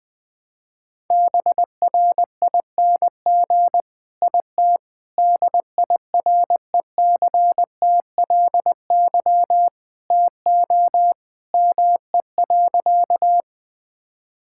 Morse code, heard as a keyed tone, about 20 wpm, 700 Hz. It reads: BRING IT DIRECTLY TO ME.
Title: Sentences from Top 1000 Words - Review Part 5 - 20wpm